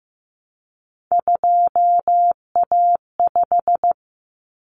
Morse code, heard as a keyed tone, about 15 wpm, 700 Hz. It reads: 2A5
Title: Common Cut Numbers - 3-Digit 15wpm